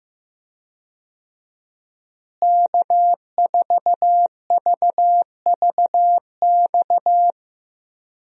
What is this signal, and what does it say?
Morse code, heard as a keyed tone, about 15 wpm, 700 Hz. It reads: K4VVX
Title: USA Call Signs - 5 Characters - Repeated 3x - 15wpm